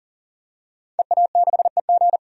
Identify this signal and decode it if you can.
Morse code, heard as a keyed tone, about 40 wpm, 700 Hz. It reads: EA6EG
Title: DXCC-100 Call Signs - 5 Characters - 40wpm